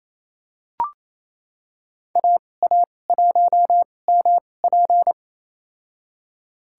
Morse code, heard as a keyed tone, about 28 wpm, spaced as for 20 wpm, 700 Hz. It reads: AA1MP
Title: USA Call Signs - 5 Characters - 2x Character Spacing - 28wpm